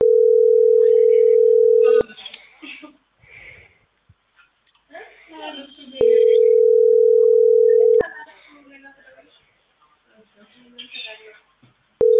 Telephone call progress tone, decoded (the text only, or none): ringback tone